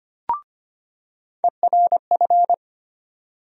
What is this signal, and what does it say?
Morse code, heard as a keyed tone, about 25 wpm, 700 Hz. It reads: ERF
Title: USA Call Sign Postfixes - 3 Letters 25wpm